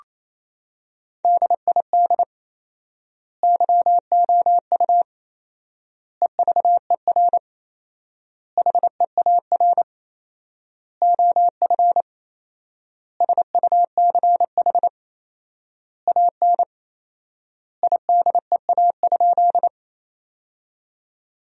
Morse code, heard as a keyed tone, about 27 wpm, 700 Hz. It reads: DID YOU EVER HEAR OF SUCH AN IDEA?
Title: 4x Word-Spacing - Sentences from Top 400 Words - 28wpm